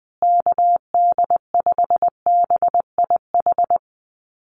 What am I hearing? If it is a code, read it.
Morse code, heard as a keyed tone, about 20 wpm, 700 Hz. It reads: KD5BIH
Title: USA Call Signs Repeated 3x 20wpm